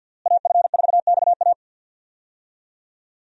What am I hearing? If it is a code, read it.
Morse code, heard as a keyed tone, about 50 wpm, 700 Hz. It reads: AF4XA
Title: USA Call Signs - 5 Characters - Repeated 2x - 50wpm